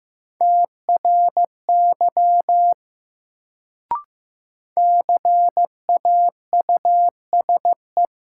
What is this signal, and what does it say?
Morse code, heard as a keyed tone, about 15 wpm, 700 Hz. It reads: TRY CAUSE
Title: Sentences from Top 200 Words - New Word Drill - 15wpm